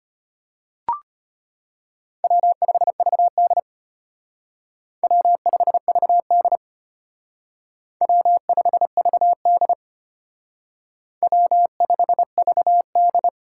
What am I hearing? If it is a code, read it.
Morse code, heard as a keyed tone, about 30 wpm, 700 Hz. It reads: W5VD W5VD W5VD W5VD
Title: Speed Racing - USA Call Signs 25wpm